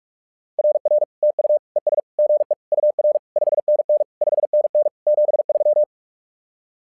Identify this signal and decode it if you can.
Morse code, heard as a keyed tone, about 44 wpm, 600 Hz. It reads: RR TU ES GE UR 5NN 5NN 73
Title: SOTA QSO Practice 45wpm